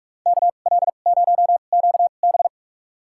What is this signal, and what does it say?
Morse code, heard as a keyed tone, about 44 wpm, spaced as for 32 wpm, 700 Hz. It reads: KL0QB
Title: USA Call Signs - 5 Characters - 2x Character Spacing - 45wpm